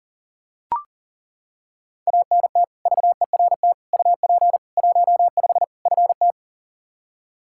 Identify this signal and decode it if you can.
Morse code, heard as a keyed tone, about 40 wpm, 700 Hz. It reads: ANT VERT UP 15 FT
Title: Practice QSO - Exchange 2 - Info - 40wpm